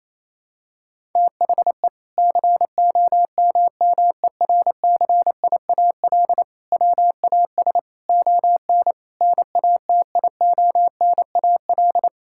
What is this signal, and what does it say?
Morse code, heard as a keyed tone, about 28 wpm, 700 Hz. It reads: THE COMMERCIAL WAS ON NATIONAL
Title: Sentences from Top 1000 Words - Review Part 5 - 28wpm